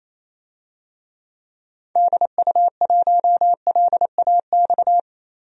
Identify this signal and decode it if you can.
Morse code, heard as a keyed tone, about 27 wpm, 700 Hz. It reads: DU1LAX
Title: DXCC-100 Call Signs - 6 Characters - 28wpm